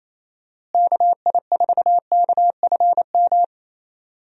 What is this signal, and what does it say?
Morse code, heard as a keyed tone, about 28 wpm, 700 Hz. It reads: KI4KFM
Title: USA Call Signs - 6 Characters - Repeated 3x - 28wpm